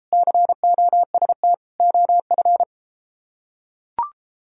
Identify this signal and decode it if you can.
Morse code, heard as a keyed tone, about 32 wpm, 700 Hz. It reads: COST OF